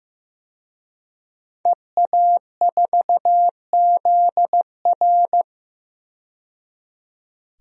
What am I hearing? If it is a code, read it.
Morse code, heard as a keyed tone, about 15 wpm, 700 Hz. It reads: EA4ZR
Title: DXCC-100 Call Signs - 5 Characters - 15wpm